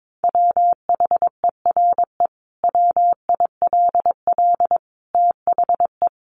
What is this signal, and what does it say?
Morse code, heard as a keyed tone, about 22 wpm, 700 Hz. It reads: WHERE WILL THE
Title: Sentences from Top 1500 Words 22wpm